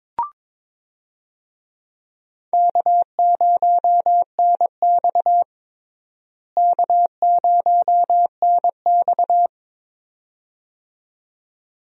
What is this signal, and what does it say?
Morse code, heard as a keyed tone, about 22 wpm, 700 Hz. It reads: K0NX K0NX
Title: USA Call Signs - 4 Characters - Repeated 2x - 22wpm